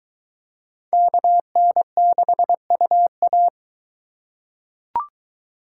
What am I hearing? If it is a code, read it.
Morse code, heard as a keyed tone, about 21 wpm, 700 Hz. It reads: KN6UA